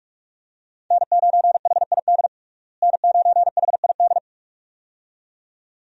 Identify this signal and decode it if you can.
Morse code, heard as a keyed tone, about 42 wpm, 700 Hz. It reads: N9HID N9HID
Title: USA Call Signs - 5 Characters - Repeated 2x - 45wpm